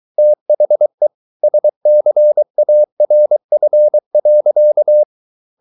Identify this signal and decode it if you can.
Morse code, heard as a keyed tone, about 23 wpm, 600 Hz. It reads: THE SCARF.